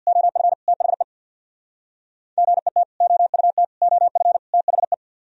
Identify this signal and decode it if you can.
Morse code, heard as a keyed tone, about 64 wpm, 700 Hz. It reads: OF THE GET OUT OF THE